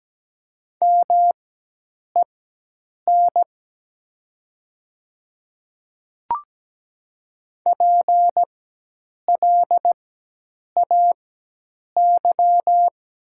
Morse code, heard as a keyed tone, about 17 wpm, spaced as for 8 wpm, 700 Hz. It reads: MEN PLAY
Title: 4x Character Spacing - Top 200 Words - 17wpm